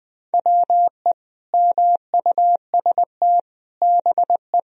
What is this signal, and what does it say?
Morse code, heard as a keyed tone, about 20 wpm, 700 Hz. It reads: WE MUST BE
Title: Sentences from Top 1000 Words - Review Part 1 - 20wpm